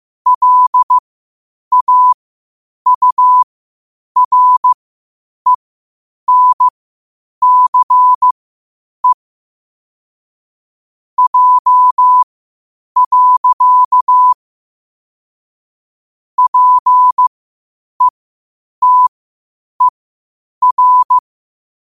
Morse code, heard as a keyed tone, about 15 wpm, spaced as for 8 wpm, 1000 Hz. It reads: LAURENCE J. PETER
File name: qotdfiles / 8wpm